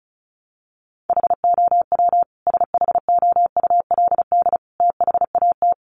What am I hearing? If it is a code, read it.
Morse code, heard as a keyed tone, about 34 wpm, 700 Hz. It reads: HOW SHOULD THAT
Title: Sentences from Top 1300 Words 35wpm